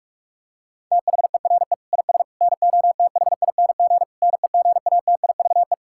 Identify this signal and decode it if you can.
Morse code, heard as a keyed tone, about 44 wpm, 700 Hz. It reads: THERE IS NOTHING NEGATIVE
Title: Sentences from Top 1000 Words - Review Part 6 - 45wpm